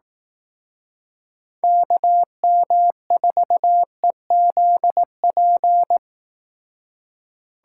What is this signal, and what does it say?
Morse code, heard as a keyed tone, about 18 wpm, 700 Hz. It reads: KM4EZP
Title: Speed Racing - USA 2x3 Call Signs 15wpm